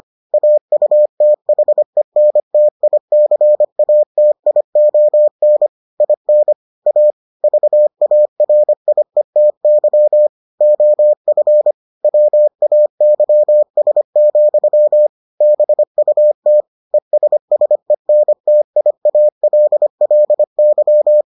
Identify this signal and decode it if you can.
Morse code, heard as a keyed tone, about 25 wpm, 600 Hz. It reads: AUTHENTICATION IN A VARIETY OF WAYS, BUT ESSENTIALLY